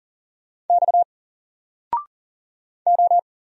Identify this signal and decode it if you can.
Morse code, heard as a keyed tone, about 40 wpm, 700 Hz. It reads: X O
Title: Single Letter-Number 40wpm